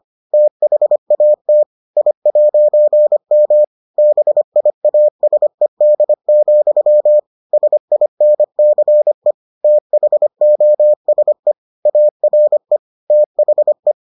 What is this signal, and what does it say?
Morse code, heard as a keyed tone, about 25 wpm, 600 Hz. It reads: THAT I'M BIASED, SINCE THOSE ARE THE